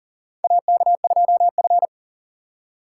Morse code, heard as a keyed tone, about 40 wpm, 700 Hz. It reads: AK2F
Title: USA Call Signs - 4x Character Spacing - 40wpm